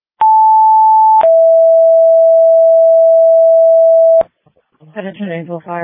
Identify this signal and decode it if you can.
Two-tone page, heard as a one-second tone, then a three-second tone, about 880 then 660 Hz, with speech. It is heard from 0.2 to 4.2 s.